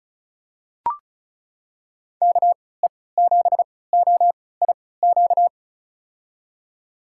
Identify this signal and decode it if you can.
Morse code, heard as a keyed tone, about 35 wpm, spaced as for 20 wpm, 690 Hz. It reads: KE7OIQ
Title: USA Call Signs - 6 Characters - 3x Character Spacing - 35wpm